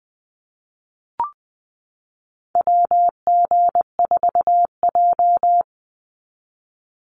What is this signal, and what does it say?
Morse code, heard as a keyed tone, about 20 wpm, 700 Hz. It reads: WG4J
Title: USA Call Signs 20wpm